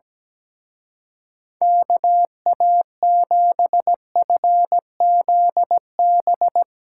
Morse code, heard as a keyed tone, about 18 wpm, 700 Hz. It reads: KA7FZB
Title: Speed Racing - USA Call Signs 17wpm